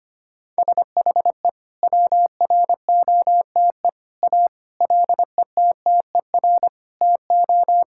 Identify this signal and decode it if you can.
Morse code, heard as a keyed tone, about 25 wpm, 700 Hz. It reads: SHE WROTE A LETTER TO
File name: practice / 25wpm / Sentences with 2-Word Phrases - 25wpm